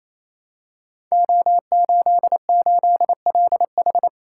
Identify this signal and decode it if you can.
Morse code, heard as a keyed tone, about 29 wpm, 700 Hz. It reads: O88LH
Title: Code Groups - 5 Letter:Numbers - 28wpm